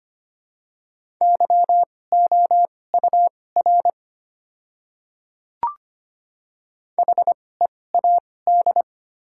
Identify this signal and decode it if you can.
Morse code, heard as a keyed tone, about 25 wpm, spaced as for 18 wpm, 700 Hz. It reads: YOUR HEAD
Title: 2x Character Spacing - Top 300 Words - 25wpm